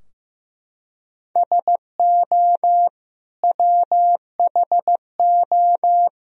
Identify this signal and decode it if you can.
Morse code, heard as a keyed tone, about 15 wpm, 700 Hz. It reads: SO WHO
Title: Sentences from Top 1000 Words - Review Part 3 - 15wpm